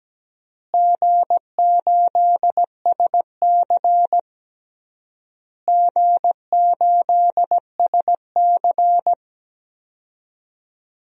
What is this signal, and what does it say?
Morse code, heard as a keyed tone, about 17 wpm, 700 Hz. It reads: G8SC G8SC
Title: DXCC-100 Call Signs - 4 Characters - Repeated 3x - 17wpm